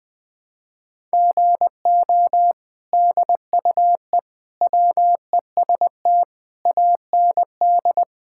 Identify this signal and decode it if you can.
Morse code, heard as a keyed tone, about 21 wpm, 700 Hz. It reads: GO DUE WEST AND
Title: Sentences from Top 1000 Words - Review Part 6 - 20wpm